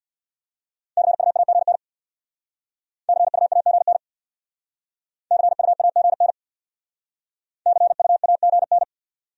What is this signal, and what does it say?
Morse code, heard as a keyed tone, about 66 wpm, 700 Hz. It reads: CUAGN CUAGN CUAGN CUAGN